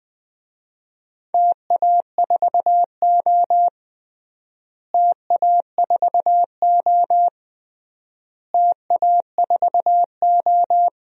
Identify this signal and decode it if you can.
Morse code, heard as a keyed tone, about 20 wpm, 700 Hz. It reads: TA4O TA4O TA4O
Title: DXCC-100 Call Signs - 4 Characters - Repeated 3x - 20wpm